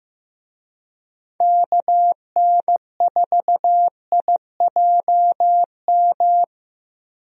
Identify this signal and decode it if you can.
Morse code, heard as a keyed tone, about 15 wpm, 700 Hz. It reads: KN4IJM
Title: USA Call Signs - 6 Characters - Repeated 2x - 15wpm